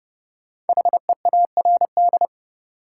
Morse code, heard as a keyed tone, about 30 wpm, 700 Hz. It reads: HEARD
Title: Sentences from Top 400 Words - New Word Drill - 30wpm